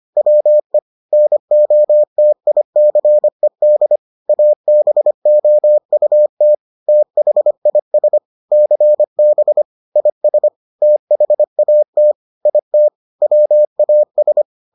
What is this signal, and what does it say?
Morse code, heard as a keyed tone, about 25 wpm, 600 Hz. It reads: WE NOTICED ABOUT THIS CB IS THAT IT WAS